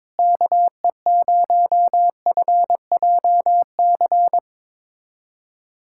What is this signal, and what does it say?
Morse code, heard as a keyed tone, about 22 wpm, 700 Hz. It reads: KE0FJC
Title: USA Callsign DE Callsign - 6 Characters - 22wpm